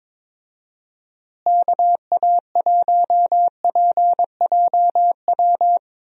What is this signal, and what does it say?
Morse code, heard as a keyed tone, about 22 wpm, 700 Hz. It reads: KA1PJW
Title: USA Call Signs - 6 Characters - Repeated 3x - 22wpm